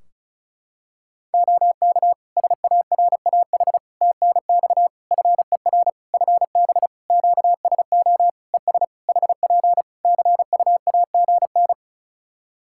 Morse code, heard as a keyed tone, about 35 wpm, 700 Hz. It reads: OK SARAH TNX FER FB QSO ES HP CUAGN
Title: Practice QSO - Exchange 4 - Ending - 35wpm